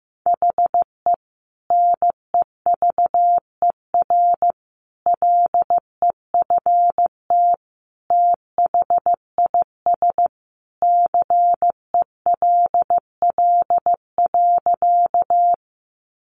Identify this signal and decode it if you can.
Morse code, heard as a keyed tone, about 15 wpm, 700 Hz. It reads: HE NEVER LEFT THIS CELL.
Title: Sentences from Top 1100 Words 15wpm